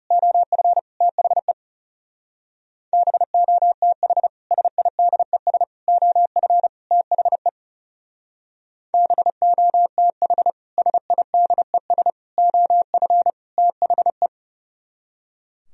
Morse code, heard as a keyed tone, about 35 wpm, 700 Hz. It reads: OF THE BOTH SIDES OF THE BOTH SIDES OF THE
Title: Speed-Racing - Sets of 4 Words 30wpm